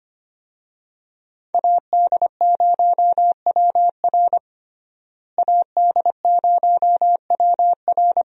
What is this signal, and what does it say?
Morse code, heard as a keyed tone, about 25 wpm, 700 Hz. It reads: AD0WR AD0WR
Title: USA Call Signs - 5 Characters - Repeated 3x - 25wpm